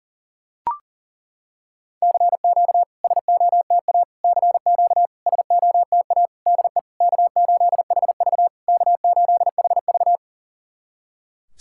Morse code, heard as a keyed tone, about 40 wpm, 700 Hz. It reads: CQ SOTA CQ SOTA DE K8HV K8HV
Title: SOTA QSO Practice 40wpm